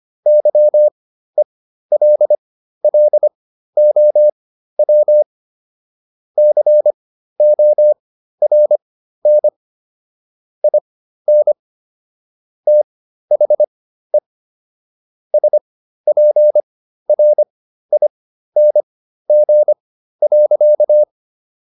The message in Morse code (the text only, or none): YELLOW CORN IN THE SPRING.